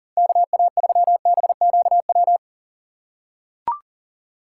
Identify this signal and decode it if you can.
Morse code, heard as a keyed tone, about 40 wpm, 700 Hz. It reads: KA3BQW
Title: USA Call Signs - 6 Characters - 4x Character Spacing - 40wpm